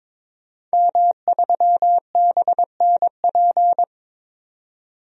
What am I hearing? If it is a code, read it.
Morse code, heard as a keyed tone, about 22 wpm, 700 Hz. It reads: M3BNP
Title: DXCC-100 Call Signs - 5 Characters - 22wpm